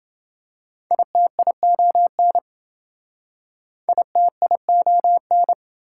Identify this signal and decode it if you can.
Morse code, heard as a keyed tone, about 28 wpm, 700 Hz. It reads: ITION ITION